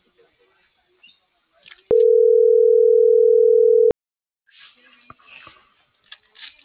Ringback tone (North America), 440 + 480 Hz.